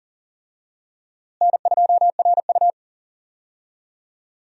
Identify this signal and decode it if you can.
Morse code, heard as a keyed tone, about 39 wpm, 700 Hz. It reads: N2RU
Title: USA Call Signs - 4 Characters - Repeated 3x - 40wpm